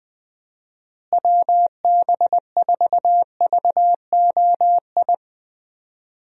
Morse code, heard as a keyed tone, about 20 wpm, 700 Hz. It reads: WB4VOI
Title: USA Call Signs - 6 Characters - Repeated 3x - 20wpm